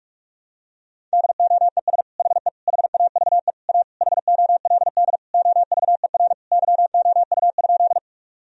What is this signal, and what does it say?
Morse code, heard as a keyed tone, about 41 wpm, 700 Hz. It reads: DOES HE HAVE A HOLD OVER YOU?
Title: Sentences from Top 1000 Words - Review Part 2 - 45wpm